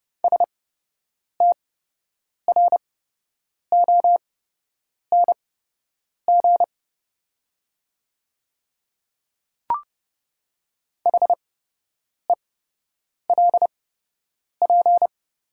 Morse code, heard as a keyed tone, about 31 wpm, spaced as for 8 wpm, 700 Hz. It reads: STRONG HELP
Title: 8x Character Spacing - Top 500 Words - 30wpm